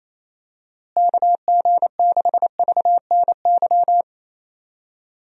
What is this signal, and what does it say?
Morse code, heard as a keyed tone, about 28 wpm, 700 Hz. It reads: KG6VNY
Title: USA Call Signs - 6 Characters - Repeated 3x - 28wpm